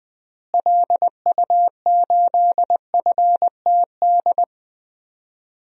Morse code, heard as a keyed tone, about 20 wpm, 700 Hz. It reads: LU8FTD
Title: DXCC-100 Call Signs - 6 Characters - Repeated 2x - 20wpm